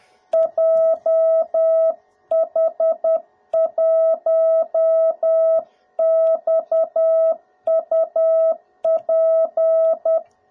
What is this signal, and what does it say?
Morse code, heard as a keyed tone, about 10 wpm, 660 Hz. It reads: JH1XUP
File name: jh8cbh / 2021cw / cw02b